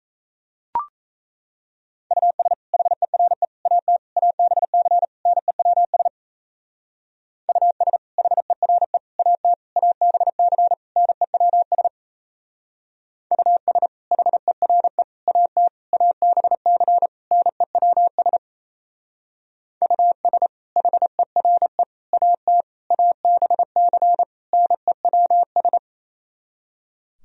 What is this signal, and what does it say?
Morse code, heard as a keyed tone, about 35 wpm, 700 Hz. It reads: US HERE AT ABC NEWS US HERE AT ABC NEWS US HERE AT ABC NEWS US HERE AT ABC NEWS
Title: Speed-Racing - Sets of 5 Words 28wpm